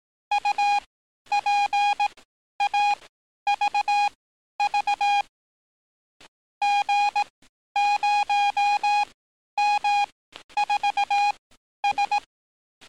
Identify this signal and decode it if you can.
Morse code, heard as a keyed tone, about 18 wpm, spaced as for 11 wpm, 820 Hz. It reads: UPAVV G0M4S
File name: training / NAVPERS 11049to11060 / NAVPERS 11050 RA RB